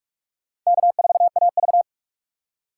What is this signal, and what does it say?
Morse code, heard as a keyed tone, about 46 wpm, 700 Hz. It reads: K4AV
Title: USA Call Signs - 4 Characters - 6x Character Spacing - 45wpm